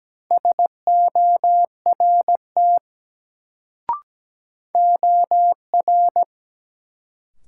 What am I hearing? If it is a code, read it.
Morse code, heard as a keyed tone, about 17 wpm, 700 Hz. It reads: SORT OR
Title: Sets of 2 Words - New Word Drill - 17wpm